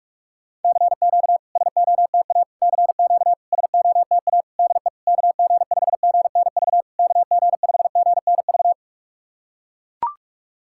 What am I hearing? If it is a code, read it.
Morse code, heard as a keyed tone, about 44 wpm, 690 Hz. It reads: CQ SOTA CQ SOTA DE KG5GNV KG5GNV
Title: SOTA QSO Practice 45wpm